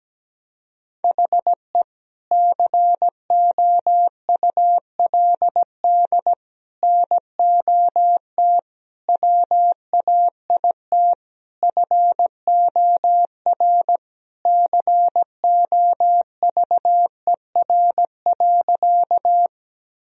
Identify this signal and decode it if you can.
Morse code, heard as a keyed tone, about 17 wpm, 700 Hz. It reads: HE COULD NOT WAIT FOR COVER.
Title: Sentences from Top 1000 Words - Review Part 1 - 17wpm